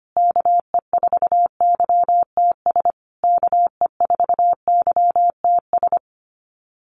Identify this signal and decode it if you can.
Morse code, heard as a keyed tone, about 25 wpm, 700 Hz. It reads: KE4YTS KE4YTS
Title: USA Call Signs Repeated 3x 25wpm